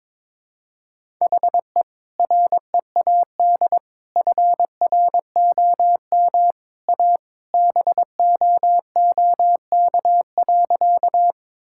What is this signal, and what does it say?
Morse code, heard as a keyed tone, about 22 wpm, 700 Hz. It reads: HE READ FROM A BOOK.
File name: practice / 22wpm / Sentences with 2-Word Phrases - 22wpm